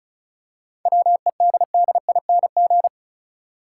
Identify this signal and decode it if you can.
Morse code, heard as a keyed tone, about 34 wpm, 700 Hz. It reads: WEDDING
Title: Seven Letter Words - 35wpm